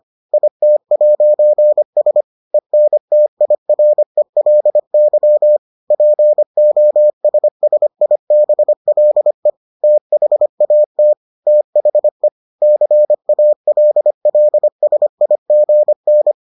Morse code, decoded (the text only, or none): IT'S ENTIRELY POSSIBLE THAT THE CALLSIGN